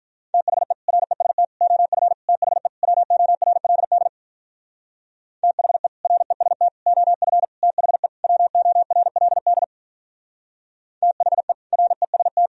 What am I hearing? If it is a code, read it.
Morse code, heard as a keyed tone, about 43 wpm, 700 Hz. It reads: THE REST OF THE WORLD THE REST OF THE WORLD THE REST
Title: Speed-Racing - Sets of 5 Words 35wpm